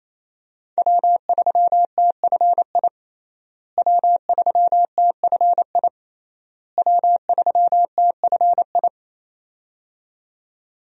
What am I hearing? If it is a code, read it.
Morse code, heard as a keyed tone, about 28 wpm, 700 Hz. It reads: W3TFI W3TFI W3TFI
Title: USA Call Signs - 5 Characters - Repeated 3x - 28wpm